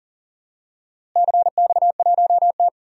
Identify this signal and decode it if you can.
Morse code, heard as a keyed tone, about 41 wpm, 700 Hz. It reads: CX1T
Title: DXCC-100 Call Signs - 4 Characters - Repeated 2x - 40wpm